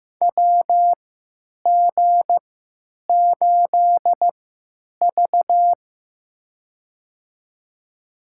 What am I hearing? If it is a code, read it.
Morse code, heard as a keyed tone, about 15 wpm, spaced as for 9 wpm, 700 Hz. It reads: WG8V
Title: USA Call Signs - 4 Characters - 3x Character Spacing - 15wpm